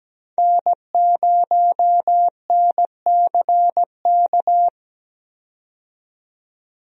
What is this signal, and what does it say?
Morse code, heard as a keyed tone, about 17 wpm, 700 Hz. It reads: N0NCK
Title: USA Call Signs - 5 Characters - 17wpm